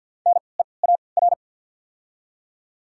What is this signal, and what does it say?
Morse code, heard as a keyed tone, about 53 wpm, spaced as for 29 wpm, 700 Hz. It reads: NEAR
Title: 3x Character Spacing - Top 200 Words - 50wpm